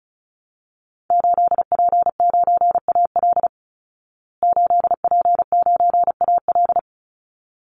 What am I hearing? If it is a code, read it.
Morse code, heard as a keyed tone, about 34 wpm, 700 Hz. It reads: 8P9AL 8P9AL
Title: Call Signs Repeated 3x - 35wpm